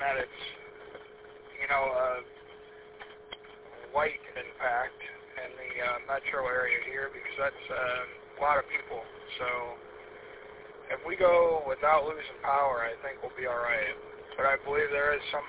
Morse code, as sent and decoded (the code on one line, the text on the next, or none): none